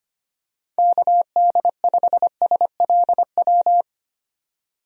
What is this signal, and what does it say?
Morse code, heard as a keyed tone, about 25 wpm, 700 Hz. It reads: KD5SLW
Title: USA Call Signs - 6 Characters - 25wpm